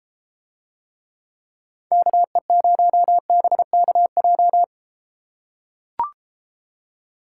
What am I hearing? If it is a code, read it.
Morse code, heard as a keyed tone, about 33 wpm, 700 Hz. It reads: KE0BKJ